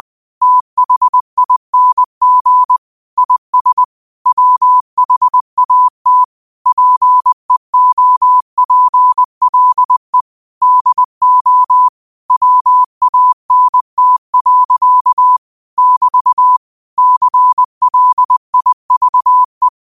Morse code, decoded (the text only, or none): THING IS WHAT PEOPLE DO WANT. <BT> CLIVE